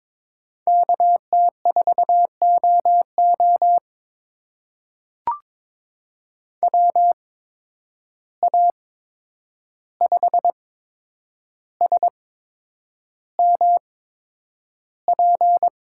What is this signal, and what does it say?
Morse code, heard as a keyed tone, about 22 wpm, spaced as for 9 wpm, 700 Hz. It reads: KT4OO WA5SMP
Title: USA Call Signs - 8x Character Spacing - 22wpm